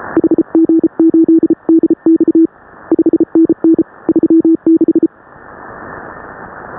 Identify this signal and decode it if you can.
Morse code, heard as a keyed tone, about 34 wpm, 310 Hz. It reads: HG8DX 5NN 36